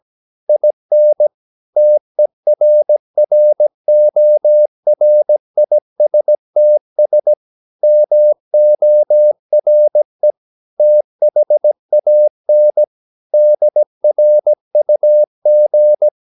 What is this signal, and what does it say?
Morse code, heard as a keyed tone, about 17 wpm, 600 Hz. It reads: IN TERRORISTS MORE THAN DRUG
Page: Romanen Never af Ken Follet på engelsk.